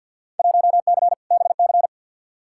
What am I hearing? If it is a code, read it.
Morse code, heard as a keyed tone, about 52 wpm, 700 Hz. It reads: 1C BC
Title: Field Day Exchanges Encore - 50wpm